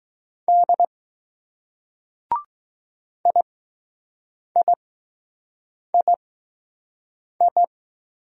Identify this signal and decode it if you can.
Morse code, heard as a keyed tone, about 20 wpm, spaced as for 6 wpm, 700 Hz. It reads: D IIII